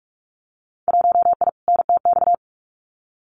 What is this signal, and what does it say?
Morse code, heard as a keyed tone, about 44 wpm, 700 Hz. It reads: 1I NTX
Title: Winter Field Day Exchanges Encore 45wpm